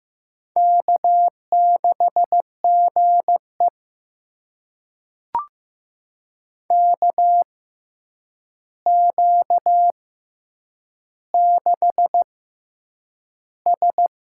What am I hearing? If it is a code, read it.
Morse code, heard as a keyed tone, about 15 wpm, spaced as for 8 wpm, 700 Hz. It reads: K6GE KQ6S
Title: USA Call Signs - 6x Character Spacing - 15wpm